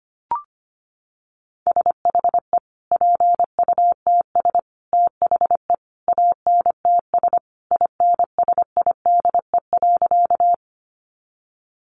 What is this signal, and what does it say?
Morse code, heard as a keyed tone, about 25 wpm, 700 Hz. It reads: SHE PUTS THE ANTS INSIDE.
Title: Additional Easy Phrases 25wpm